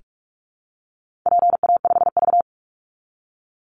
Morse code, heard as a keyed tone, about 44 wpm, 700 Hz. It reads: PA5V
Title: Call Signs Repeated 3x - 45wpm